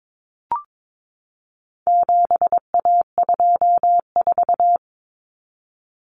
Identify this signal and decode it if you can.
Morse code, heard as a keyed tone, about 22 wpm, 700 Hz. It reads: 7A24
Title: Common Cut Numbers - 4-Digit 22wpm